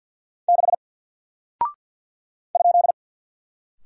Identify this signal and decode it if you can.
Morse code, heard as a keyed tone, about 48 wpm, 700 Hz. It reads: 6 ?
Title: Single Character (Letter:Number:Punctuation:Prosign) Alt 50wpm